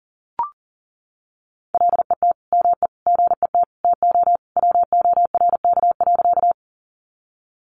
Morse code, heard as a keyed tone, about 40 wpm, 700 Hz. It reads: LET ME GET TO WORK.
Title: Sentences from Top 300 Words 40wpm